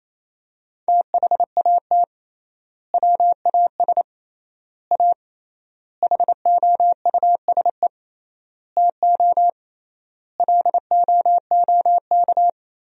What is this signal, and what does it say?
Morse code, heard as a keyed tone, about 27 wpm, 700 Hz. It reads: THAT WAS A HOUSE TO LOOK
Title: 3x Word-Spacing - Sentences from Top 200 Words - 28wpm